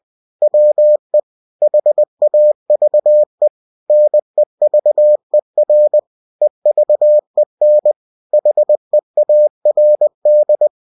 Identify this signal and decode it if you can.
Morse code, heard as a keyed tone, about 20 wpm, 600 Hz. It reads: WE HAVE NEVER EVEN HEARD